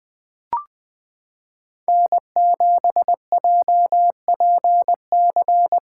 Morse code, heard as a keyed tone, about 20 wpm, 700 Hz. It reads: N7JPC
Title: USA Callsign DE Callsign - 5 Characters - 20wpm